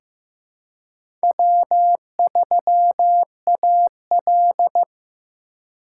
Morse code, heard as a keyed tone, about 15 wpm, 700 Hz. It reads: W3AL
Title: USA Call Signs - 4 Characters - 2x Character Spacing - 15wpm